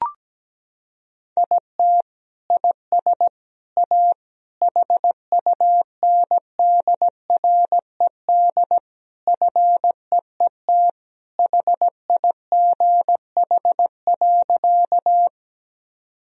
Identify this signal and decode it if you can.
Morse code, heard as a keyed tone, about 17 wpm, 700 Hz. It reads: IT IS A HUNDRED FEET HIGH.
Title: Sentences from Top 1000 Words - Review Part 2 - 17wpm